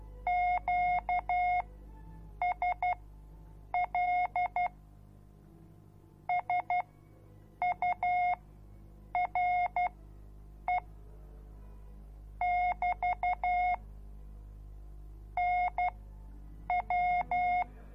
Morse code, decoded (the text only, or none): QSL SURE <BT> NW